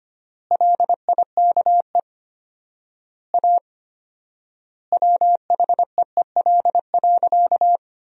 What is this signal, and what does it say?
Morse code, heard as a keyed tone, about 25 wpm, 700 Hz. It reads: LIKE A WHEEL.
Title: 4x Word-Spacing - Sentences from Top 500 Words - 25wpm